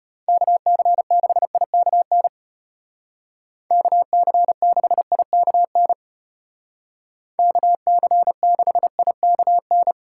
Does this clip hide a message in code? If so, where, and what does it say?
Yes: Morse code from 0.3 to 9.9 s, heard as a keyed tone, about 32 wpm, 700 Hz. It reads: KC6IKN KC6IKN KC6IKN